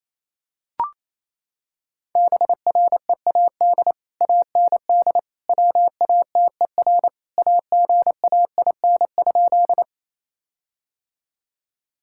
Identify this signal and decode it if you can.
Morse code, heard as a keyed tone, about 28 wpm, 700 Hz. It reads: BREAD AND WATER AGAIN?
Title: Sentences from Top 1600 Words 28wpm